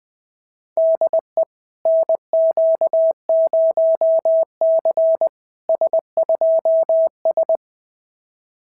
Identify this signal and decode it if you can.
Morse code, heard as a keyed tone, about 21 wpm, 660 Hz. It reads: DE NQ0C S2S
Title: SOTA QSO Practice 20wpm